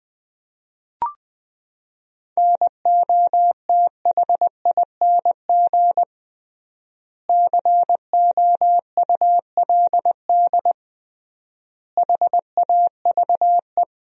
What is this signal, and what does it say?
Morse code, heard as a keyed tone, about 20 wpm, 700 Hz. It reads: NOTHING COULD HAVE
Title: 3x Word-Spacing - Sentences from Top 500 Words - 20wpm